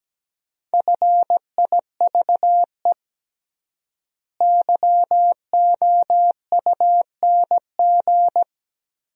Morse code, heard as a keyed tone, about 17 wpm, 700 Hz. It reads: FIVE YOUNG
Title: 3x Word-Spacing - Sentences from Top 400 Words - 17wpm